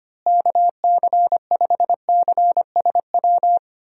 Morse code, heard as a keyed tone, about 25 wpm, 700 Hz. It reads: KC5CSW
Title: USA Call Signs - 6 Characters - Repeated 3x - 25wpm